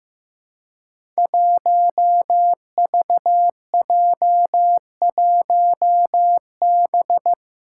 Morse code, heard as a keyed tone, about 15 wpm, 700 Hz. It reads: 1VJ1B
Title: Five Challenging Letter-Number Combinations 15wpm